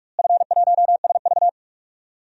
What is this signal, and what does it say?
Morse code, heard as a keyed tone, about 47 wpm, 700 Hz. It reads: F1SV
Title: DXCC-100 Call Signs - 4 Characters - Repeated 2x - 45wpm